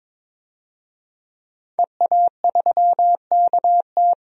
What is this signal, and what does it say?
Morse code, heard as a keyed tone, about 21 wpm, 700 Hz. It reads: EA3KT
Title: DXCC-100 Call Signs - 5 Characters - 22wpm